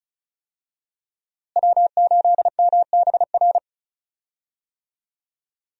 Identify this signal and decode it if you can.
Morse code, heard as a keyed tone, about 34 wpm, 700 Hz. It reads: W8MBR
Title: USA Call Signs - 5 Characters - Repeated 2x - 35wpm